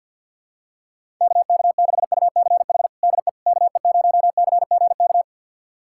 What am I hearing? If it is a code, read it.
Morse code, heard as a keyed tone, about 49 wpm, 700 Hz. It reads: KK6UCH DE KE0CGK
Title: USA Callsign DE Callsign 50wpm